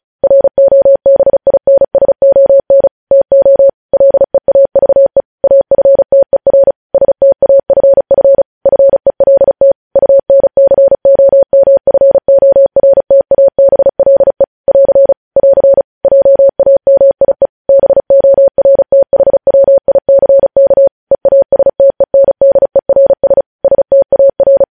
Morse code, heard as a keyed tone, about 36 wpm, 550 Hz. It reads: ROBINSON TO LEAVE AFTER STAFF FELT UNCOMFORTABLE <AR> <AR> JAMIE BORTHWICK EASTENDERS STAR
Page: news headlines in morse code at 35 wpm